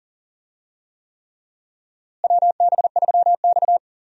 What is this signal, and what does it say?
Morse code, heard as a keyed tone, about 40 wpm, 700 Hz. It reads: WB3X
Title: USA Call Signs - 4 Characters - Repeated 3x - 40wpm